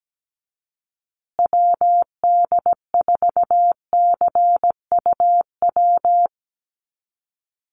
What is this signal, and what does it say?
Morse code, heard as a keyed tone, about 17 wpm, 700 Hz. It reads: WD4CUW
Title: USA Call Signs Repeated 3x 17wpm